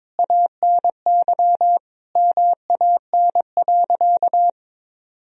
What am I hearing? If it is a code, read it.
Morse code, heard as a keyed tone, about 22 wpm, 690 Hz. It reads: ANY MAN.
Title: Sentences from Top 1000 Words - Review Part 3 - 22wpm